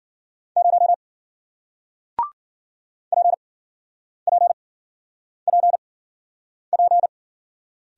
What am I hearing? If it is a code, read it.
Morse code, heard as a keyed tone, about 50 wpm, spaced as for 9 wpm, 700 Hz. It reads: 0 PPPP
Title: Speed-Racing Single Letter-Number 40wpm